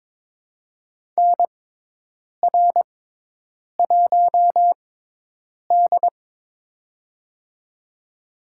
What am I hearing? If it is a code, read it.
Morse code, heard as a keyed tone, about 23 wpm, spaced as for 8 wpm, 700 Hz. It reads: NR1D
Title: USA Call Signs - 4 Characters - 6x Character Spacing - 22wpm